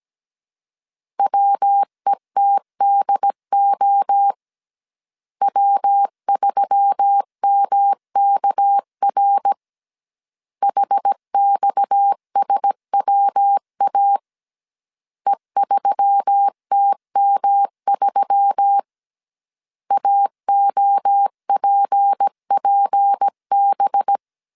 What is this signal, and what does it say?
Morse code, heard as a keyed tone, about 17 wpm, 800 Hz. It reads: WETDO W3MKR HXSWA E3TM3 AOPPB
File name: CW / DET / 1 6 1500C S B 16wpm 2619